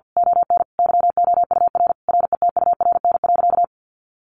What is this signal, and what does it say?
Morse code, heard as a keyed tone, about 50 wpm, 700 Hz. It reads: ON YOUR RETURN.
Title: A Sound Of Thunder 50wpm